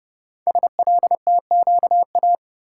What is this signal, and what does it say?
Morse code, heard as a keyed tone, about 29 wpm, 700 Hz. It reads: SLTQA
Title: Code Groups - 5 Letters - 30wpm